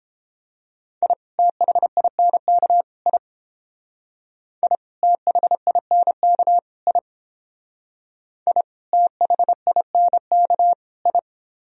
Morse code, heard as a keyed tone, about 30 wpm, 700 Hz. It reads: I THINK I I THINK I I THINK I